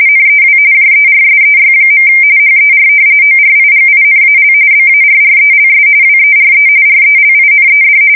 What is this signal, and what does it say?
RTTY, about 45.45 baud. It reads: NEW JERSEY. HER ORIGINAL BARRELS WERE RELINED
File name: fsk170 / battleshipnj